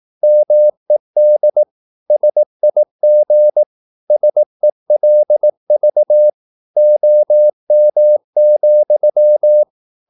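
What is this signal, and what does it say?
Morse code, heard as a keyed tone, about 18 wpm, 600 Hz. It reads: MED SIG SELV OM,
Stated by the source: Slow (QRS) Morse practice at 18 wpm, CW med Gnister.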